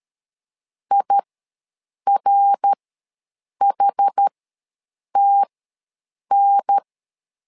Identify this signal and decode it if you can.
Morse code, heard as a keyed tone, about 12 wpm, spaced as for 7 wpm, 800 Hz. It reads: IRHTN
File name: LEZIONI / 3 MWRO / 3 MWR ws 12wpm 1809